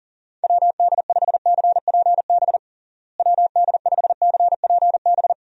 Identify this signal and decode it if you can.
Morse code, heard as a keyed tone, about 40 wpm, 700 Hz. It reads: WD5CPB WD5CPB
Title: USA Call Signs - 6 Characters - Repeated 2x - 40wpm